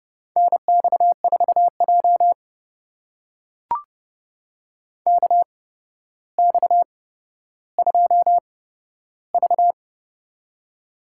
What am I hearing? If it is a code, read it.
Morse code, heard as a keyed tone, about 31 wpm, spaced as for 13 wpm, 700 Hz. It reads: NX4J KX2V
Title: USA Call Signs - 4 Characters - 8x Character Spacing - 30wpm